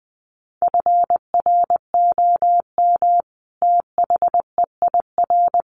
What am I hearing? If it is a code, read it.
Morse code, heard as a keyed tone, about 20 wpm, 700 Hz. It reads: FROM THEIR
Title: A Sound Of Thunder 20wpm